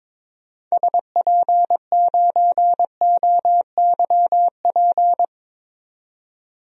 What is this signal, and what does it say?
Morse code, heard as a keyed tone, about 22 wpm, 700 Hz. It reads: SP9OYP
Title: DXCC-100 Call Signs - 6 Characters - 22wpm